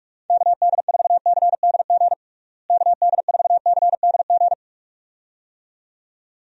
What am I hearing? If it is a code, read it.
Morse code, heard as a keyed tone, about 46 wpm, 700 Hz. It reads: KD4CDG KD4CDG
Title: USA Call Signs - 6 Characters - Repeated 2x - 45wpm